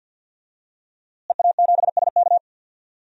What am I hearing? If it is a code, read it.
Morse code, heard as a keyed tone, about 52 wpm, 700 Hz. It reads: EA7SK
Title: DXCC-100 Call Signs - 5 Characters - Repeated 3x - 50wpm